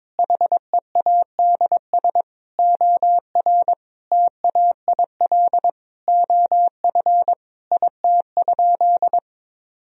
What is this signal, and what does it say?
Morse code, heard as a keyed tone, about 22 wpm, 700 Hz. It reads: HEADS OR TAIL OF IT?
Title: Sentences from Top 1000 Words - Review Part 3 - 22wpm